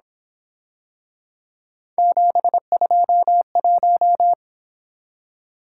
Morse code, heard as a keyed tone, about 27 wpm, 700 Hz. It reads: 721
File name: Speed-Racing - Three Letter-Number Combinations - 22wpm